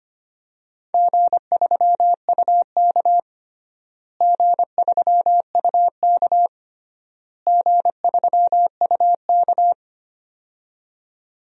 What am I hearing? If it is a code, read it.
Morse code, heard as a keyed tone, about 25 wpm, 700 Hz. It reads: G3UK G3UK G3UK
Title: DXCC-100 Call Signs - 4 Characters - Repeated 3x - 25wpm